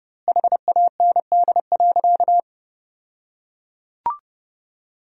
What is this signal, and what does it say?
Morse code, heard as a keyed tone, about 30 wpm, 700 Hz. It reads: HAND.
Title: 2x Word-Spacing - Sentences from Top 500 Words - 30wpm